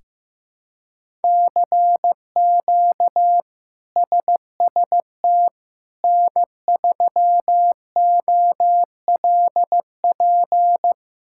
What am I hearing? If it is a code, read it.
Morse code, heard as a keyed tone, about 16 wpm, 700 Hz. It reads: CQ SST N3OLP
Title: K1USN SST QSO Practice - 15wpm